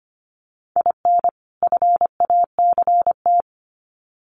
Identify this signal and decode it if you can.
Morse code, heard as a keyed tone, about 24 wpm, 700 Hz. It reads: IN FACT
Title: Sets of 2 Words 25wpm